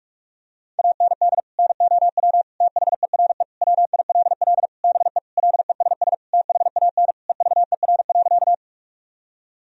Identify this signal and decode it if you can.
Morse code, heard as a keyed tone, about 41 wpm, 700 Hz. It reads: AND NOW THERE WILL BE LESS THAN EVER.
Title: Sentences from Top 1000 Words - Review Part 2 - 45wpm